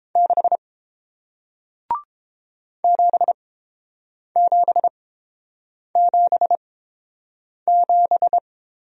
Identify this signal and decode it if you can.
Morse code, heard as a keyed tone, about 30 wpm, spaced as for 7 wpm, 700 Hz. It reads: 6 7777